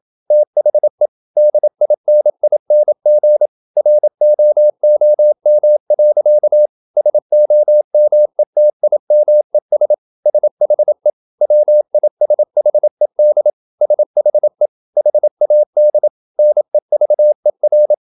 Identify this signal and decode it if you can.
Morse code, heard as a keyed tone, about 27 wpm, 600 Hz. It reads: THE DINING ROOM. SOMETIMES SHE WISHED SHE HAD NEVER
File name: Never 0027wpm